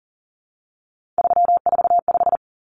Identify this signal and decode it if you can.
Morse code, heard as a keyed tone, about 40 wpm, 700 Hz. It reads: 345
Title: Common Cut Numbers - 3-Digit 40wpm